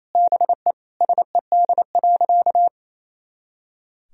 Morse code, heard as a keyed tone, about 28 wpm, 700 Hz. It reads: BE SED.
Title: Sentences from Top 100 Words and Abbreviations 28wpm